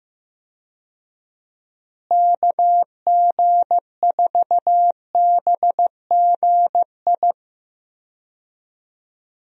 Morse code, heard as a keyed tone, about 15 wpm, 700 Hz. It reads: KG4BGI
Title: USA Call Signs - 6 Characters - Repeated 2x - 15wpm